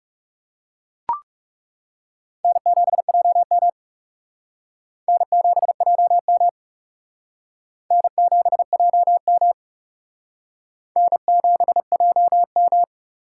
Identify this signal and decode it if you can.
Morse code, heard as a keyed tone, about 37 wpm, 700 Hz. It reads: N7JM N7JM N7JM N7JM
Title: Speed Racing - USA 1x2 Call Signs 30wpm